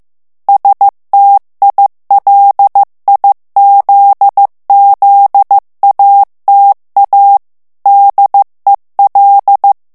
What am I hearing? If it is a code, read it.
Morse code, heard as a keyed tone, about 15 wpm, 800 Hz. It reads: STILIZZATA DEL
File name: ITALIEN texte clair 1 vitesse 14 mots minute